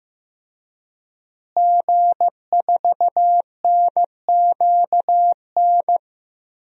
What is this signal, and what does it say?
Morse code, heard as a keyed tone, about 15 wpm, 700 Hz. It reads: G4NQN
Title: DXCC-100 Call Signs - 5 Characters - Repeated 3x - 15wpm